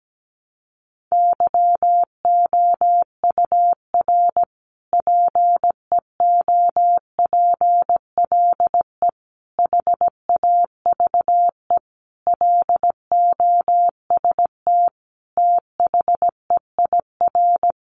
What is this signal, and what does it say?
Morse code, heard as a keyed tone, about 17 wpm, 700 Hz. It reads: YOUR PEOPLE HAVE LOST THEIR
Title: Sentences from Top 1400 Words 17wpm